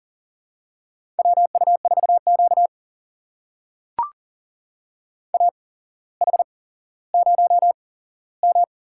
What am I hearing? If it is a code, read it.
Morse code, heard as a keyed tone, about 40 wpm, spaced as for 17 wpm, 700 Hz. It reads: WU4Q AH0M
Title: USA Call Signs - 4 Characters - 8x Character Spacing - 40wpm